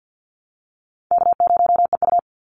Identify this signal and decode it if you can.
Morse code, heard as a keyed tone, about 50 wpm, 700 Hz. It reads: K0EU
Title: Call Signs Repeated 2x - 50wpm